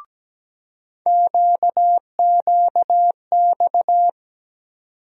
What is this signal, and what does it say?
Morse code, heard as a keyed tone, about 17 wpm, 700 Hz. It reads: QQX
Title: USA Call Sign Postfixes - 3 Letters 17wpm